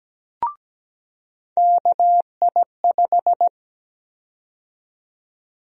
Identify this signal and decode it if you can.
Morse code, heard as a keyed tone, about 17 wpm, 700 Hz. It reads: KI5
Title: USA Call Sign Prefixes - 17wpm